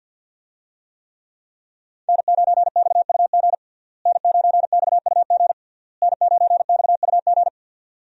Morse code, heard as a keyed tone, about 47 wpm, 700 Hz. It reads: N9XUG N9XUG N9XUG
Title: USA Call Signs - 5 Characters - Repeated 3x - 50wpm